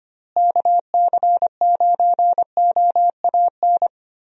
Morse code, heard as a keyed tone, about 25 wpm, 700 Hz. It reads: KC9OAN
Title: USA Call Signs - 6 Characters - Repeated 3x - 25wpm